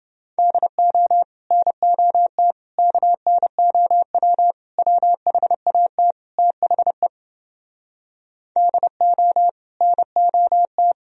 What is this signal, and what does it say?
Morse code, heard as a keyed tone, about 30 wpm, 700 Hz. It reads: DO NOT KNOW WHAT THE DO NOT